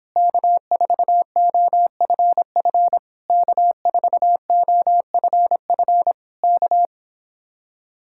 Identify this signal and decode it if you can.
Morse code, heard as a keyed tone, about 26 wpm, 700 Hz. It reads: K4OFF K4OFF K